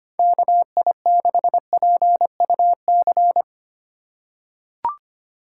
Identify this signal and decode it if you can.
Morse code, heard as a keyed tone, about 25 wpm, 700 Hz. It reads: KI6PUC
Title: USA Call Signs - 6 Characters - 4x Character Spacing - 25wpm